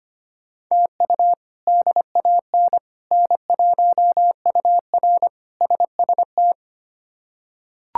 Morse code, heard as a keyed tone, about 25 wpm, 700 Hz. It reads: TU DAN N1UR SST
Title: K1USN SST QSO Practice - 25wpm